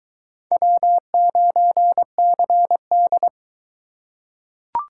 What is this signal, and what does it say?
Morse code, heard as a keyed tone, about 23 wpm, 700 Hz. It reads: W9CD